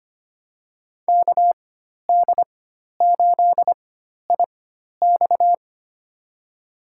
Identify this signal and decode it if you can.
Morse code, heard as a keyed tone, about 25 wpm, spaced as for 12 wpm, 700 Hz. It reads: KD8IX
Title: USA Call Signs - 4x Character Spacing - 25wpm